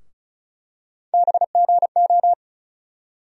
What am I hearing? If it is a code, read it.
Morse code, heard as a keyed tone, about 36 wpm, 700 Hz. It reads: BGO
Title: USA Call Sign Postfixes - 3 Letters 35wpm